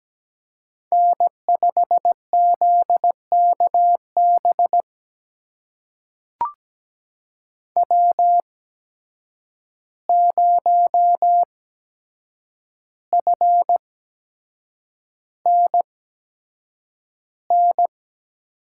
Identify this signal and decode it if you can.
Morse code, heard as a keyed tone, about 17 wpm, spaced as for 7 wpm, 700 Hz. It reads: N5ZKB W0FNN
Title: USA Call Signs - 5 Characters - 8x Character Spacing - 17wpm